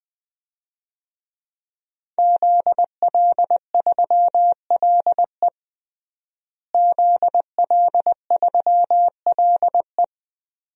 Morse code, heard as a keyed tone, about 20 wpm, 700 Hz. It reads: ZL3LE ZL3LE
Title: DXCC-100 Call Signs - 5 Characters - Repeated 2x - 20wpm